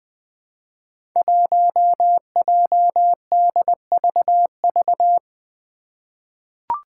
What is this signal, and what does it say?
Morse code, heard as a keyed tone, about 20 wpm, 700 Hz. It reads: 1JDVV
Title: Five Challenging Letter-Number Combinations 20wpm